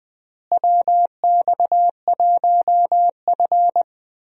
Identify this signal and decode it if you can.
Morse code, heard as a keyed tone, about 20 wpm, 700 Hz. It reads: WX1F
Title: USA Call Signs - 4 Characters - Repeated 3x - 20wpm